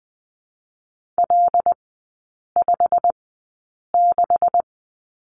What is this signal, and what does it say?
Morse code, heard as a keyed tone, about 20 wpm, spaced as for 8 wpm, 690 Hz. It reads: L56
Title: 172 Lesson 28e - Number 9 - Three Characters 20wpm